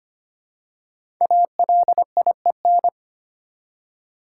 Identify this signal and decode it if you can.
Morse code, heard as a keyed tone, about 25 wpm, 700 Hz. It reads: ALIEN
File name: 5 Letter Words - First 3 Letters Given - 25wpm